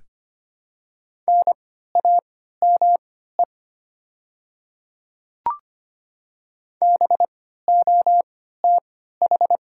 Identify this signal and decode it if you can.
Morse code, heard as a keyed tone, about 25 wpm, spaced as for 14 wpm, 690 Hz. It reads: NAME BOTH
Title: 3x Character Spacing - Top 300 Words - 25wpm